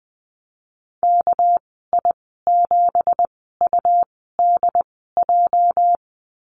Morse code, heard as a keyed tone, about 20 wpm, spaced as for 14 wpm, 700 Hz. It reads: KI7UDJ
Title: 207 Lesson 33g - Letter j - Call Signs 20wpm